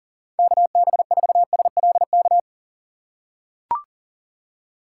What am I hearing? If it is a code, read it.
Morse code, heard as a keyed tone, about 38 wpm, 700 Hz. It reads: KB4SLK
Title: USA Call Signs - 6 Characters - 4x Character Spacing - 40wpm